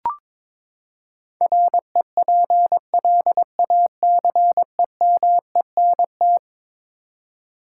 Morse code, heard as a keyed tone, about 22 wpm, 700 Hz. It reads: REPLACEMENT
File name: practice / 22wpm / Eleven Letter Words - 22wpm